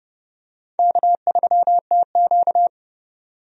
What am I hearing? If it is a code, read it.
Morse code, heard as a keyed tone, about 30 wpm, 700 Hz. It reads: K3TQ
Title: USA Call Signs - 4 Characters - 30wpm